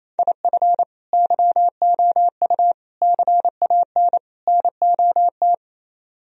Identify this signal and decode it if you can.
Morse code, heard as a keyed tone, about 34 wpm, 700 Hz. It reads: IF YOU CAN NOT
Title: Speed-Racing - Sets of 4 Words 28wpm